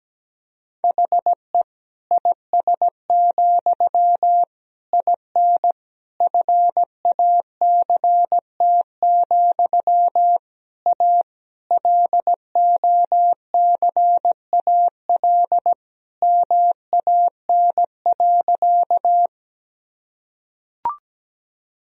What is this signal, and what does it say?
Morse code, heard as a keyed tone, about 17 wpm, 700 Hz. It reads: HE IS, IN FACT, A LOCAL MAN.
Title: Sentences from Top 1000 Words - Review Part 3 - 17wpm